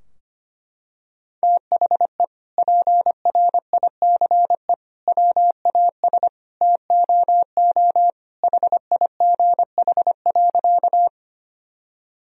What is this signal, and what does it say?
Morse code, heard as a keyed tone, about 25 wpm, 700 Hz. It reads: THE PRICE WAS TOO HIGH.
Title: Sentences from Top 1000 Words - Review Part 4 - 25wpm